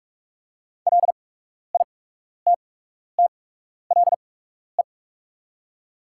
Morse code, heard as a keyed tone, about 46 wpm, spaced as for 12 wpm, 700 Hz. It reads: LITTLE
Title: 8x Character Spacing - Top 500 Words - 45wpm